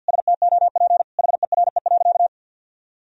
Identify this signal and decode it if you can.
Morse code, heard as a keyed tone, about 52 wpm, 700 Hz. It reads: STOP HERE.
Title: Sentences from Top 1000 Words - Review Part 1 - 50wpm